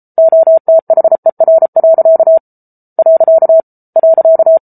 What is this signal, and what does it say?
Morse code, heard as a keyed tone, about 34 wpm, 650 Hz. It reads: OTHER. .